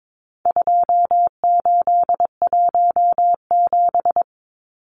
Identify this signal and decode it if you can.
Morse code, heard as a keyed tone, about 22 wpm, 700 Hz. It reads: 2817
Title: Numbers - 4-Digit 22wpm